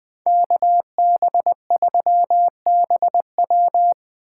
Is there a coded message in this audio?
Yes: Morse code from 0.3 to 3.9 s, heard as a keyed tone, about 21 wpm, 700 Hz. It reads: KB3BW